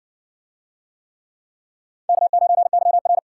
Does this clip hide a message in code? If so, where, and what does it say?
Yes: Morse code from 2.1 to 3.2 s, heard as a keyed tone, about 60 wpm, 700 Hz. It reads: K9YR